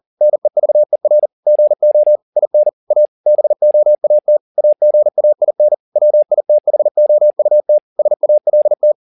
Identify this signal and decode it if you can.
Morse code, heard as a keyed tone, about 40 wpm, 600 Hz. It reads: NEVER GO IN A BOAT AGAIN WITHOUT SALT
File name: the old man and the sea 2 40wpm